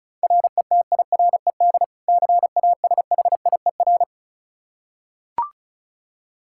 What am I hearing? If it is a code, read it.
Morse code, heard as a keyed tone, about 35 wpm, 700 Hz. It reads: RETIRED CASHIER
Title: K1USN SST QSO Practice 35wpm